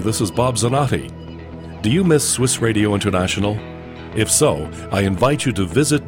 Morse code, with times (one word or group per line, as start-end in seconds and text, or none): none